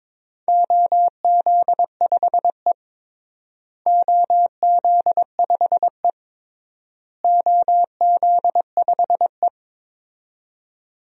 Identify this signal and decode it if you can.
Morse code, heard as a keyed tone, about 22 wpm, 700 Hz. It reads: OZ5E OZ5E OZ5E
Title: DXCC-100 Call Signs - 4 Characters - Repeated 3x - 22wpm